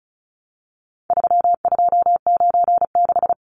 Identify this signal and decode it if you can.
Morse code, heard as a keyed tone, about 34 wpm, 700 Hz. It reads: 3296
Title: Numbers - 4-Digit 35wpm